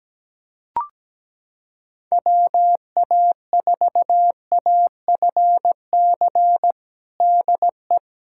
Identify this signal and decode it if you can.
Morse code, heard as a keyed tone, about 17 wpm, 700 Hz. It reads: WA4AFC DE
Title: USA Callsign DE Callsign - 6 Characters - 17wpm